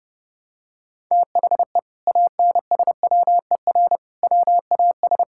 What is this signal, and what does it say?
Morse code, heard as a keyed tone, about 32 wpm, 700 Hz. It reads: THE ANSWER WAS
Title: Sentences from Top 1000 Words - Review Part 6 - 30wpm